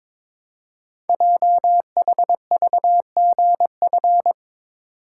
Morse code, heard as a keyed tone, about 22 wpm, 700 Hz. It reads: JHVGF
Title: Code Groups - 5 Letters - 22wpm